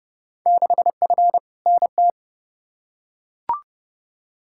Morse code, heard as a keyed tone, about 30 wpm, 700 Hz. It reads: 6F NT
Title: Field Day Exchanges Encore - 30wpm